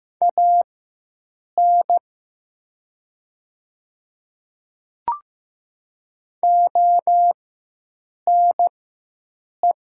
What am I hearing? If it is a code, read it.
Morse code, heard as a keyed tone, about 15 wpm, spaced as for 7 wpm, 700 Hz. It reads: AN ONE
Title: 4x Character Spacing - Top 100 Words - 15wpm